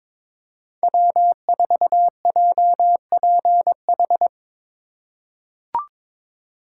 Morse code, heard as a keyed tone, about 22 wpm, 700 Hz. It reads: W4JPH
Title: K1USN SST QSO Practice - 22wpm